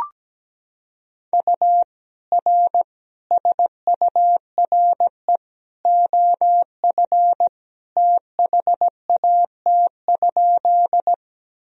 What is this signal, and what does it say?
Morse code, heard as a keyed tone, about 17 wpm, 700 Hz. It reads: U R SURE OF THAT?
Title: Sentences from Top 300 Words and Abbreviations 17wpm